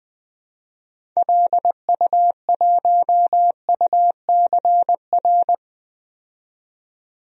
Morse code, heard as a keyed tone, about 20 wpm, 700 Hz. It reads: LU1UCR
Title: DXCC-100 Call Signs - 6 Characters - 20wpm